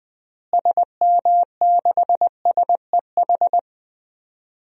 Morse code, heard as a keyed tone, about 20 wpm, 700 Hz. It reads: SM6SEH
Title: DXCC-100 Call Signs - 6 Characters - Repeated 2x - 20wpm